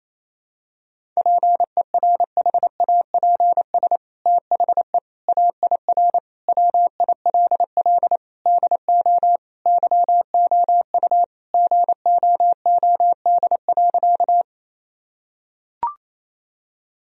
Morse code, heard as a keyed tone, about 28 wpm, 700 Hz. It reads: PERHAPS THE AIR WILL DO YOU GOOD.
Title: Sentences from Top 1000 Words - Review Part 1 - 28wpm